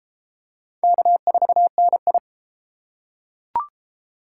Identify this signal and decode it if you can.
Morse code, heard as a keyed tone, about 28 wpm, 700 Hz. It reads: K4NI